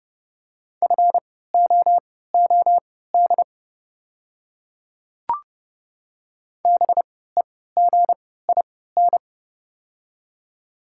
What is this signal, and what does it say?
Morse code, heard as a keyed tone, about 32 wpm, spaced as for 17 wpm, 690 Hz. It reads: FOOD BEGIN
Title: 3x Character Spacing - Top 300 Words - 30wpm